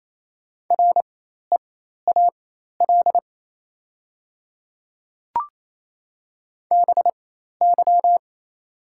Morse code, heard as a keyed tone, about 27 wpm, spaced as for 13 wpm, 690 Hz. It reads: REAL BY
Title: 4x Character Spacing - Top 300 Words - 28wpm